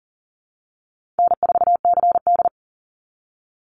Morse code, heard as a keyed tone, about 40 wpm, 700 Hz. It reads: N4CD
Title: Easy Call Signs Repeated 2x - 40wpm